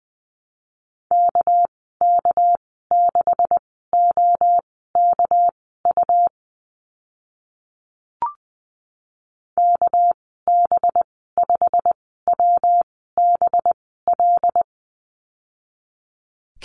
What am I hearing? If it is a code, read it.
Morse code, heard as a keyed tone, about 20 wpm, spaced as for 15 wpm, 700 Hz. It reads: KK6OKU KB5WBL
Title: 245 Lesson 39g - Letter z - Call Signs 20wpm